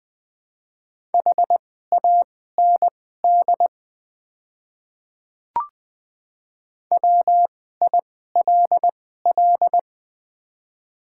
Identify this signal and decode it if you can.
Morse code, heard as a keyed tone, about 20 wpm, spaced as for 14 wpm, 700 Hz. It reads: HAND WILL
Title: 2x Character Spacing - Top 200 Words - 20wpm